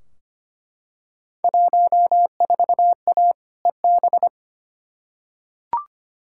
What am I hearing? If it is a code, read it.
Morse code, heard as a keyed tone, about 25 wpm, 700 Hz. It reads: 14A EB
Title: Field Day Exchanges Encore - 25wpm